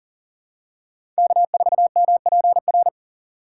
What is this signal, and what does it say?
Morse code, heard as a keyed tone, about 37 wpm, 700 Hz. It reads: K4MPR
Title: USA Call Signs - 5 Characters - 2x Character Spacing - 40wpm